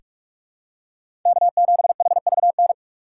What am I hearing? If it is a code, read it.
Morse code, heard as a keyed tone, about 47 wpm, 700 Hz. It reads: K7HVN
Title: USA Call Signs - 5 Characters - 3x Character Spacing - 45wpm